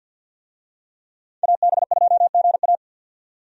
Morse code, heard as a keyed tone, about 50 wpm, 700 Hz. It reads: AB2GA
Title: USA Call Signs - 5 Characters - 50wpm